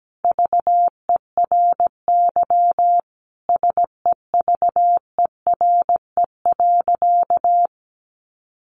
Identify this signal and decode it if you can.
Morse code, heard as a keyed tone, about 17 wpm, 700 Hz. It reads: VERY SEVERE.
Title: Sentences from Top 1400 Words 17wpm